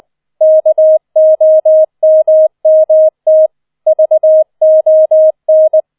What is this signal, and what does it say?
Morse code, heard as a keyed tone, about 20 wpm, 630 Hz. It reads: KOMMT VON